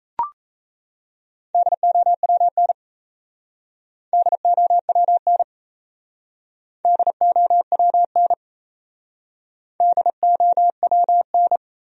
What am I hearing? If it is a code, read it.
Morse code, heard as a keyed tone, about 36 wpm, 700 Hz. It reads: DOWN DOWN DOWN DOWN